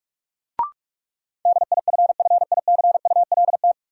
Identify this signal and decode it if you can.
Morse code, heard as a keyed tone, about 46 wpm, 700 Hz. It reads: DIFFICULT
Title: Sentences from Top 600 Words - New Word Drill - 45wpm